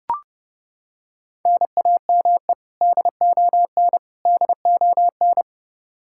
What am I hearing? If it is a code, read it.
Morse code, heard as a keyed tone, about 32 wpm, 700 Hz. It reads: NAME DON DON
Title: Practice QSO - Exchange 1 - Beginning - 30wpm